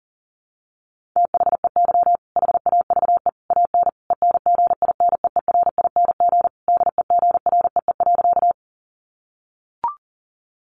Morse code, heard as a keyed tone, about 40 wpm, 700 Hz. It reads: THEY HAVE AN ENGINEERING DEGREE.
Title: Sentences from Top 1500 Words 40wpm